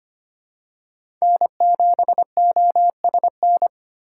Morse code, heard as a keyed tone, about 25 wpm, 700 Hz. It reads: N7OSN
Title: USA Call Signs - 5 Characters - 2x Character Spacing - 25wpm